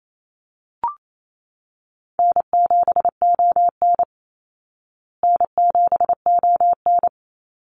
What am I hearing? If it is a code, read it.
Morse code, heard as a keyed tone, about 28 wpm, 700 Hz. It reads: N7ON N7ON
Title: Call Signs Repeated 3x - 28wpm